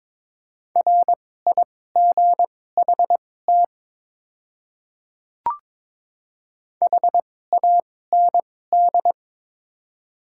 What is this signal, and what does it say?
Morse code, heard as a keyed tone, about 21 wpm, spaced as for 16 wpm, 700 Hz. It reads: RIGHT HAND
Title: 2x Character Spacing - Top 300 Words - 22wpm